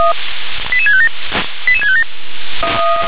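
Five-tone selective call (ZVEI), heard as quick runs of five tones, tones of about 70 ms. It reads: 88657 88657
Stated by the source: Received on SDR as a ZVEI five-tone selective call, address 88655.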